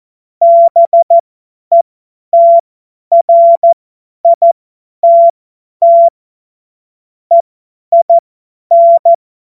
Morse code, heard as a keyed tone, about 14 wpm, spaced as for 10 wpm, 680 Hz. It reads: BETRITT EIN
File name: Morsekurs / Witze / Witz 09 April 2024 50 F70